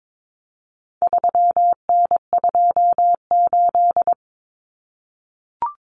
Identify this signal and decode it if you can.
Morse code, heard as a keyed tone, about 22 wpm, 700 Hz. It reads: 3N28
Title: Common Cut Numbers - 4-Digit 22wpm